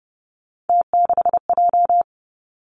Morse code, heard as a keyed tone, about 30 wpm, 700 Hz. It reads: T6J
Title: Three Challenging Letter-Number Combinations 30wpm